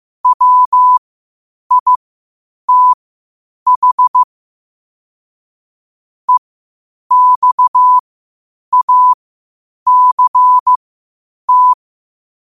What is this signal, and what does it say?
Morse code, heard as a keyed tone, about 15 wpm, spaced as for 8 wpm, 1000 Hz. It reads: WITH EXACT
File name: qotdfiles / 8wpm